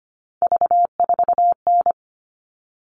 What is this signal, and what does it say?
Morse code, heard as a keyed tone, about 25 wpm, 700 Hz. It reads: V4N
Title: All Cut Numbers - 3-Digit 25wpm